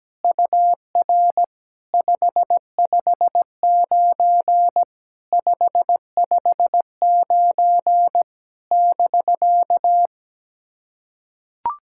Morse code, heard as a keyed tone, about 17 wpm, 700 Hz. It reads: UR 559 559 <BK>
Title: SOTA QSO Practice 17wpm